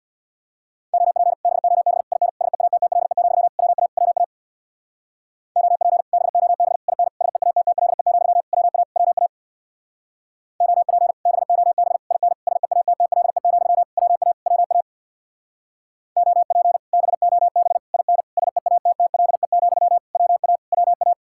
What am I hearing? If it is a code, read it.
Morse code, heard as a keyed tone, about 61 wpm, 700 Hz. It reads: OP BOB IN SEATTLE, WA WA OP BOB IN SEATTLE, WA WA OP BOB IN SEATTLE, WA WA OP BOB IN SEATTLE, WA WA
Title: Speed-Racing Common QSO Phrases Encore - 50wpm